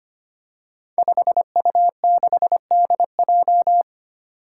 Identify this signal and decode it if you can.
Morse code, heard as a keyed tone, about 25 wpm, 700 Hz. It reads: 5U6DJ
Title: Five Challenging Letter-Number Combinations 25wpm